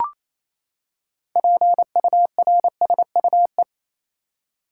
Morse code, heard as a keyed tone, about 28 wpm, 700 Hz. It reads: PURSUE
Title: Six Letter Words - 28wpm